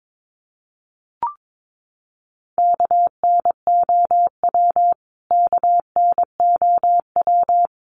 Morse code, heard as a keyed tone, about 22 wpm, 700 Hz. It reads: KNOW KNOW
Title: Top 200 Words Repeated 22wpm